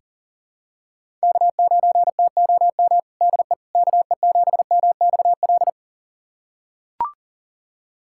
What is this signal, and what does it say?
Morse code, heard as a keyed tone, about 40 wpm, 700 Hz. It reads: K9TOM DE KE7MXL
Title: USA Callsign DE Callsign 40wpm